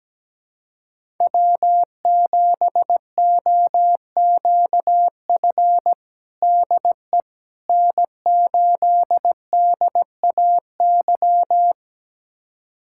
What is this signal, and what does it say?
Morse code, heard as a keyed tone, about 17 wpm, 700 Hz. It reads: W7OQF DE N8DAY
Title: USA Callsign DE Callsign - 5 Characters - 17wpm